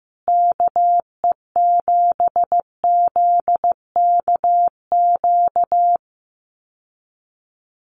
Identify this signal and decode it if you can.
Morse code, heard as a keyed tone, about 15 wpm, 700 Hz. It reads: KE7ZKQ
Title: USA Call Signs Repeated 2x 15wpm